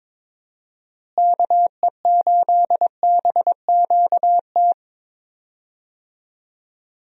Morse code, heard as a keyed tone, about 22 wpm, 700 Hz. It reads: KE8BQT
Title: USA Call Signs - 6 Characters - Repeated 2x - 22wpm